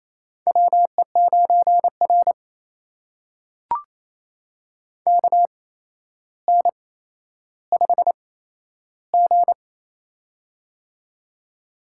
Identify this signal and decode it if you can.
Morse code, heard as a keyed tone, about 27 wpm, spaced as for 12 wpm, 700 Hz. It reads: WE9R KN5G
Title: USA Call Signs - 4 Characters - 8x Character Spacing - 28wpm